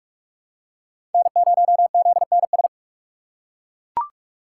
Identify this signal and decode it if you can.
Morse code, heard as a keyed tone, about 41 wpm, 700 Hz. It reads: N0ZNS
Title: USA Call Signs - 6x Character Spacing - 45wpm